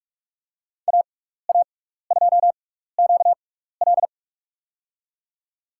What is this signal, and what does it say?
Morse code, heard as a keyed tone, about 46 wpm, spaced as for 16 wpm, 700 Hz. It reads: AA2QL
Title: USA Call Signs - 5 Characters - 6x Character Spacing - 45wpm